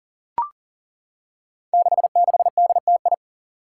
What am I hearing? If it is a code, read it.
Morse code, heard as a keyed tone, about 40 wpm, 700 Hz. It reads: 66DTI
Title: Five Challenging Letter-Number Combinations 40wpm